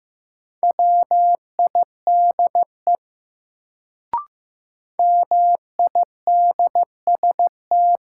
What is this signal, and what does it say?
Morse code, heard as a keyed tone, about 15 wpm, 700 Hz. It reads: WIDE MIDST
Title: Sets of 4 Words - New Word Drill - 15wpm